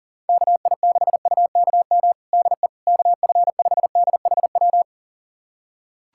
Morse code, heard as a keyed tone, about 40 wpm, 700 Hz. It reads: KI6UKM DE KF5DHW
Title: USA Callsign DE Callsign - 6 Characters - 40wpm